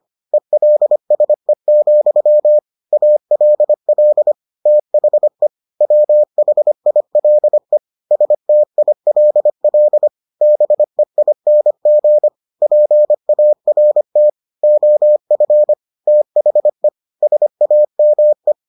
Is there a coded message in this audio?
Yes: Morse code from 0.3 to 18.5 s, heard as a keyed tone, about 25 wpm, 600 Hz. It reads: ELSE, ALL THE WHILE STILL BEING PART OF THE SAME